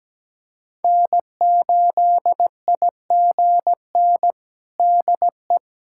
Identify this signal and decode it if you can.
Morse code, heard as a keyed tone, about 17 wpm, 700 Hz. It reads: N8IGN DE
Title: USA Callsign DE Callsign - 5 Characters - 17wpm